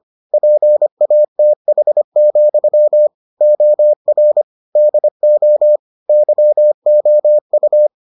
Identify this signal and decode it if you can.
Morse code, heard as a keyed tone, about 25 wpm, 600 Hz. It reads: PATH, OR DO YOU